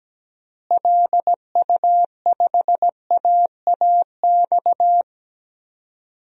Morse code, heard as a keyed tone, about 17 wpm, 700 Hz. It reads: LU5AAX
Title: DXCC-100 Call Signs - 6 Characters - Repeated 3x - 17wpm